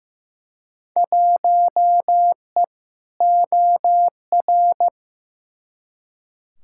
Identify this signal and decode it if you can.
Morse code, heard as a keyed tone, about 15 wpm, 700 Hz. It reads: 1E OR
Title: Field Day Exchanges Encore - 15wpm